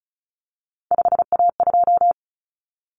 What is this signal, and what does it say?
Morse code, heard as a keyed tone, about 34 wpm, 700 Hz. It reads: 5A2
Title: Common Cut Numbers - 3-Digit 35wpm